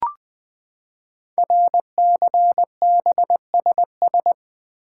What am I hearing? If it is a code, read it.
Morse code, heard as a keyed tone, about 20 wpm, 700 Hz. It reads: RCBSS
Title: Code Groups - 5 Letters - 20wpm